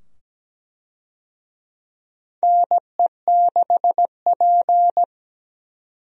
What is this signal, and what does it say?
Morse code, heard as a keyed tone, about 17 wpm, 700 Hz. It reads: NE6P
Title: USA Call Signs - 4 Characters - 17wpm